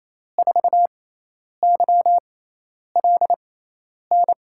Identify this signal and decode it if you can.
Morse code, heard as a keyed tone, about 28 wpm, spaced as for 10 wpm, 700 Hz. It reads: 4YLN
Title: USA Call Signs - 6 Characters - 6x Character Spacing - 28wpm